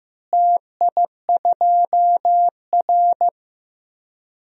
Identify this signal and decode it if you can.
Morse code, heard as a keyed tone, about 15 wpm, 700 Hz. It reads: TI2R
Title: DXCC-100 Call Signs - 4 Characters - Repeated 2x - 15wpm